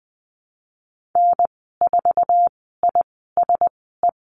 Morse code, heard as a keyed tone, about 20 wpm, spaced as for 14 wpm, 700 Hz. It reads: N4ISE
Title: 069 Lesson 12g - Letter d - Call Signs 20wpm